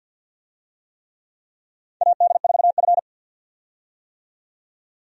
Morse code, heard as a keyed tone, about 48 wpm, 700 Hz. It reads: AD4F
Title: USA Call Signs - 4 Characters - 50wpm